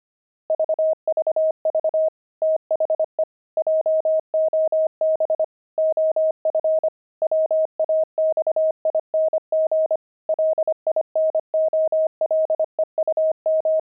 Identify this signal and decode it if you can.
Morse code, heard as a keyed tone, about 25 wpm, 630 Hz. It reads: VVV THE JOB OF WAXING LINOLEUM
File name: pangram-fs-25-cs-25-linoleum